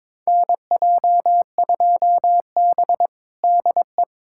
Morse code, heard as a keyed tone, about 22 wpm, 700 Hz. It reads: NJ2B DE
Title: USA Callsign DE Callsign - 4 Characters - 22wpm